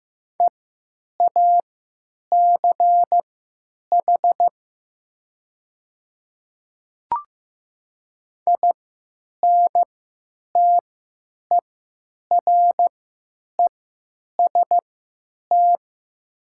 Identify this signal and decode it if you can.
Morse code, heard as a keyed tone, about 15 wpm, spaced as for 9 wpm, 700 Hz. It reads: EACH INTEREST
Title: 3x Character Spacing - Top 500 Words - 15wpm